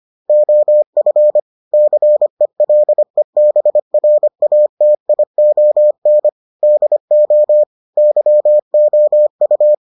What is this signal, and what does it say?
Morse code, heard as a keyed tone, about 25 wpm, 600 Hz. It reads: OF CELEBRATION DO YOU